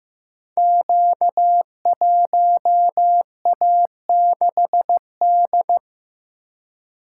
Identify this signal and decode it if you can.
Morse code, heard as a keyed tone, about 15 wpm, 700 Hz. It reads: Q1A6D
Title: Code Groups - 5 Letter:Numbers - 15wpm